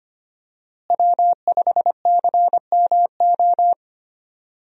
Morse code, heard as a keyed tone, about 25 wpm, 700 Hz. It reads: W5CMO
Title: USA Call Signs - 5 Characters - Repeated 2x - 25wpm